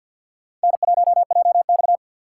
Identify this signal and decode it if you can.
Morse code, heard as a keyed tone, about 52 wpm, 700 Hz. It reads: N1JX
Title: USA Call Signs - 4 Characters - 2x Character Spacing - 50wpm